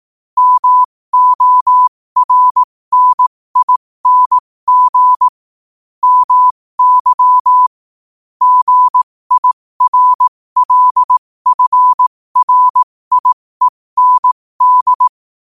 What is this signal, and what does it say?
Morse code, heard as a keyed tone, about 18 wpm, spaced as for 15 wpm, 1000 Hz. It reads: MORNING MY GIRLFRIEND
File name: qotdfiles / 15wpm